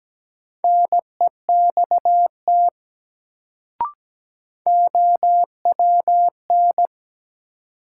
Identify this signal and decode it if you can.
Morse code, heard as a keyed tone, about 17 wpm, 700 Hz. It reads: NEXT OWN
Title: Sets of 2 Words - New Word Drill - 17wpm